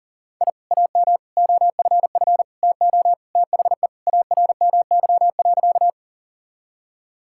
Morse code, heard as a keyed tone, about 38 wpm, 700 Hz. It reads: I AM OFF TO THE ARMY.
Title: Sentences from Top 1000 Words - Review Part 5 - 40wpm